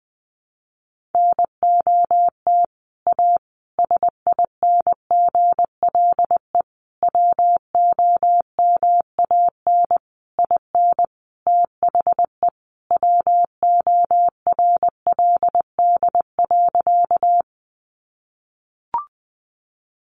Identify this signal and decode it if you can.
Morse code, heard as a keyed tone, about 20 wpm, 700 Hz. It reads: NOT A SINGLE WOMAN IN THE WORLD.
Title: Sentences from Top 1100 Words 20wpm